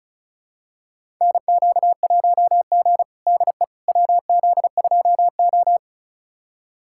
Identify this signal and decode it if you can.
Morse code, heard as a keyed tone, about 36 wpm, 690 Hz. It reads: NQ1G DE WZ2O
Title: USA Callsign DE Callsign - 4 Characters - 35wpm